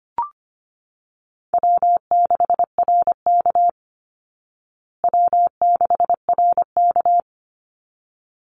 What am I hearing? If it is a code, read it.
Morse code, heard as a keyed tone, about 25 wpm, 700 Hz. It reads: W6RK W6RK
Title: Call Signs Repeated 3x - 25wpm